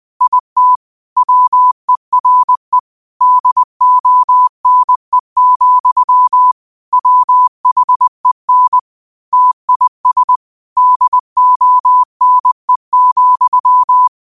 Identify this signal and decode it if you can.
Morse code, heard as a keyed tone, about 20 wpm, 1000 Hz. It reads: IT WERE DONE, WHEN TIS DONE,